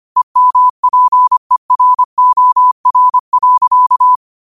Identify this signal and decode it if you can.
Morse code, heard as a keyed tone, about 25 wpm, 1000 Hz. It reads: EMPEROR.
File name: qotdfiles / 25wpm